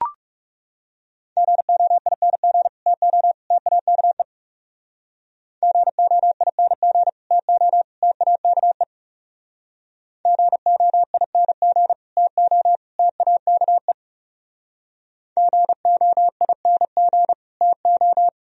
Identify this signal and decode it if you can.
Morse code, heard as a keyed tone, about 40 wpm, 700 Hz. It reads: GOING TO TAKE GOING TO TAKE GOING TO TAKE GOING TO